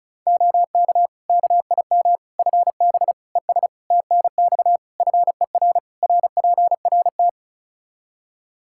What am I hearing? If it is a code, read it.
Morse code, heard as a keyed tone, about 36 wpm, 700 Hz. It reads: OK KIM FB ES TNX FER RPRT
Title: Practice QSO - Exchange 2 - Info - 35wpm